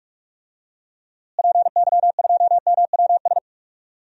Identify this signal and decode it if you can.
Morse code, heard as a keyed tone, about 46 wpm, 700 Hz. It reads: PY2MWS
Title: DXCC-100 Call Signs - 6 Characters - Repeated 2x - 45wpm